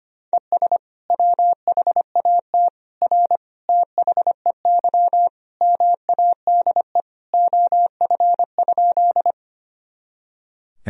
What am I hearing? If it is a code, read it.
Morse code, heard as a keyed tone, about 25 wpm, 700 Hz. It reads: ES WHAT R THEY MADE OF?
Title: Sentences from Top 200 Words and Abbreviations 25wpm